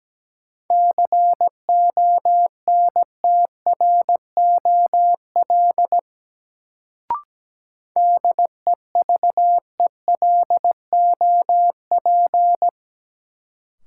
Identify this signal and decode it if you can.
Morse code, heard as a keyed tone, about 17 wpm, 700 Hz. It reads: CONTROL DEVELOP
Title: Sentences from Top 1000 Words - Review Part 2 - New Word Drill - 17wpm